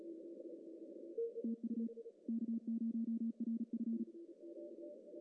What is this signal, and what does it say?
Morse code, heard as a keyed tone, about 34 wpm, 230 Hz. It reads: TU K0RF
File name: CW / cqww-cw-k5tr